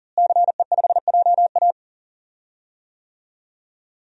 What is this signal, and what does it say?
Morse code, heard as a keyed tone, about 40 wpm, 700 Hz. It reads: CE5JA
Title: DXCC-100 Call Signs - 5 Characters - Repeated 3x - 40wpm